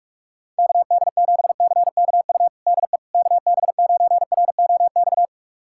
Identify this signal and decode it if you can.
Morse code, heard as a keyed tone, about 46 wpm, 690 Hz. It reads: KD7CKU DE KB9ROX
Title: USA Callsign DE Callsign 45wpm